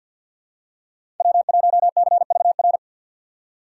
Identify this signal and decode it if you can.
Morse code, heard as a keyed tone, about 53 wpm, 700 Hz. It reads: W1CVR
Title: USA Call Signs - 5 Characters - 4x Character Spacing - 50wpm